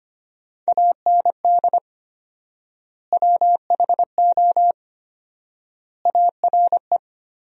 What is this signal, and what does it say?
Morse code, heard as a keyed tone, about 25 wpm, 700 Hz. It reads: AND WHO ARE
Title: 4x Word-Spacing - Sentences from Top 100 Words - 25wpm